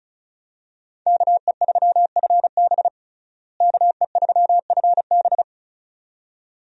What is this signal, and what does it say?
Morse code, heard as a keyed tone, about 34 wpm, 700 Hz. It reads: KE3FB KE3FB
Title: USA Call Signs - 5 Characters - Repeated 2x - 35wpm